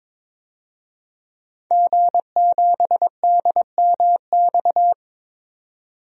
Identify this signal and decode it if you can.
Morse code, heard as a keyed tone, about 22 wpm, 700 Hz. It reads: G7DMX
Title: DXCC-100 Call Signs - 5 Characters - 22wpm